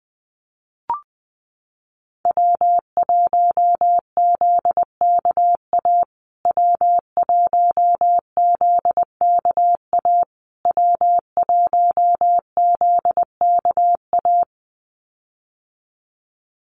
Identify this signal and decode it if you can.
Morse code, heard as a keyed tone, about 20 wpm, 700 Hz. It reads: W1ZKA W1ZKA W1ZKA
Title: USA Call Signs Repeated 3x 20wpm